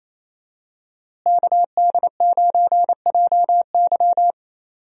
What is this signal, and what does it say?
Morse code, heard as a keyed tone, about 28 wpm, 700 Hz. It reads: KD9JY
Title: USA Call Signs - 5 Characters - Repeated 2x - 28wpm